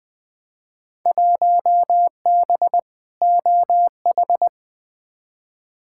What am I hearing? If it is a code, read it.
Morse code, heard as a keyed tone, about 20 wpm, 700 Hz. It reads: 1B OH
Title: Field Day Exchanges - 20wpm